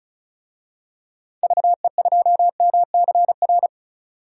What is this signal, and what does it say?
Morse code, heard as a keyed tone, about 36 wpm, 700 Hz. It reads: VE2MCR
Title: DXCC-100 Call Signs - 6 Characters - 35wpm